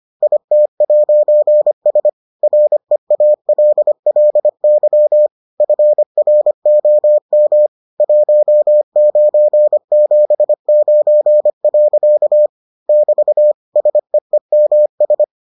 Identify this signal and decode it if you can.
Morse code, heard as a keyed tone, about 25 wpm, 600 Hz. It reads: IT'S REALLY FROM 1979. <BT> SEEMS